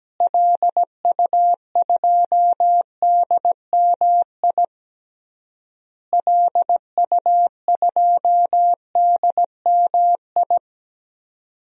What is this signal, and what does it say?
Morse code, heard as a keyed tone, about 17 wpm, 700 Hz. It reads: LU2DMI LU2DMI
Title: DXCC-100 Call Signs - 6 Characters - Repeated 3x - 17wpm